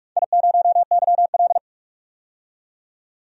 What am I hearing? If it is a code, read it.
Morse code, heard as a keyed tone, about 44 wpm, 700 Hz. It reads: I0YL
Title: DXCC-100 Call Signs - 4 Characters - 45wpm